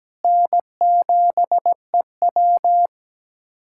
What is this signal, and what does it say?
Morse code, heard as a keyed tone, about 17 wpm, 700 Hz. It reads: N7EW
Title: USA Callsign DE Callsign - 4 Characters - 17wpm